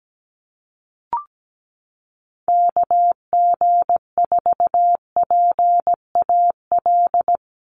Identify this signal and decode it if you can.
Morse code, heard as a keyed tone, about 17 wpm, 700 Hz. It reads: KG4PAL
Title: USA Call Signs 17wpm